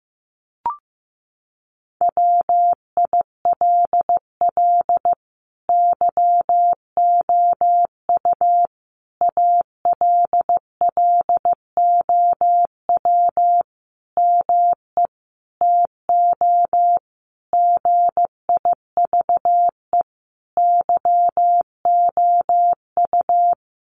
Morse code, heard as a keyed tone, about 15 wpm, 700 Hz. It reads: WILL YOU ALLOW ME TO GIVE YOU
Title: Sentences from Top 1200 Words 15wpm